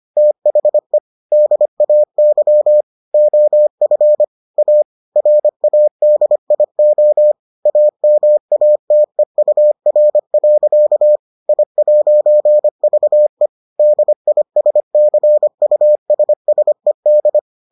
Morse code, heard as a keyed tone, about 25 wpm, 600 Hz. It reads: THE DAY OF A RADIO AMATEUR. I'VE DISCUSSED